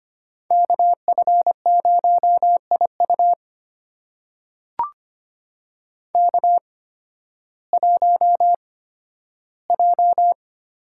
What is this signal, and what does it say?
Morse code, heard as a keyed tone, about 25 wpm, spaced as for 13 wpm, 700 Hz. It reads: KF0IU K1J
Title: USA Call Signs - 8x Character Spacing - 25wpm